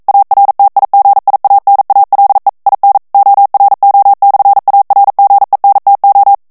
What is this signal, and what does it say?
Morse code, heard as a keyed tone, about 42 wpm, 800 Hz. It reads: ARTIGIANALE IN ORO/ARGENTO
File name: ITALIEN texte clair 1 vitesse 40 mots minute